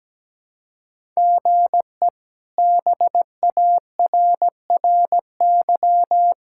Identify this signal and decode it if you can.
Morse code, heard as a keyed tone, about 17 wpm, 700 Hz. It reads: GE BARRY
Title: K1USN SST QSO Practice - 17wpm